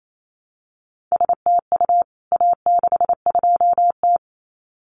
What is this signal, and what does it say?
Morse code, heard as a keyed tone, about 28 wpm, 700 Hz. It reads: STU A62T
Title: CWT Exchange 28wpm